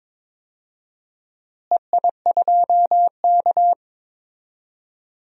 Morse code, heard as a keyed tone, about 21 wpm, 700 Hz. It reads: EI2K
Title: DXCC-100 Call Signs - 4 Characters - Repeated 3x - 22wpm